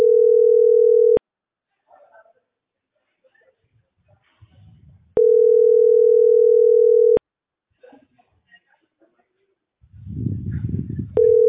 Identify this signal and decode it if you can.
Ringback tone (North America), 440 + 480 Hz.